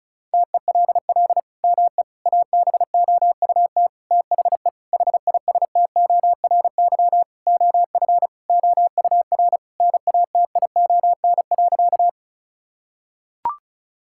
Morse code, heard as a keyed tone, about 36 wpm, 700 Hz. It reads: TELL ME ABOUT THE HISTORY OF OUR NATION.
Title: Sentences from Top 1000 Words - Review Part 6 - 35wpm